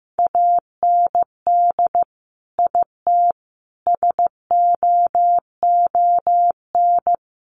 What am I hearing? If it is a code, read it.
Morse code, heard as a keyed tone, about 15 wpm, 700 Hz. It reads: AND IT SOON
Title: Sentences from Top 900 Words 15wpm